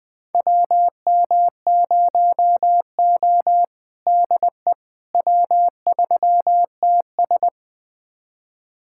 Morse code, heard as a keyed tone, about 20 wpm, 700 Hz. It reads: WM0O DE W3TS
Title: USA Callsign DE Callsign - 4 Characters - 20wpm